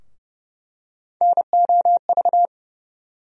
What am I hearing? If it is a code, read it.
Morse code, heard as a keyed tone, about 30 wpm, 700 Hz. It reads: NOV
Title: USA Call Sign Postfixes - 3 Letters 30wpm